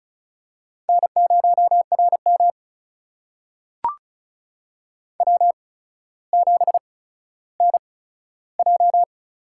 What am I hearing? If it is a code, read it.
Morse code, heard as a keyed tone, about 36 wpm, spaced as for 15 wpm, 700 Hz. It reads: N0RM W7NJ
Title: USA Call Signs - 4 Characters - 8x Character Spacing - 35wpm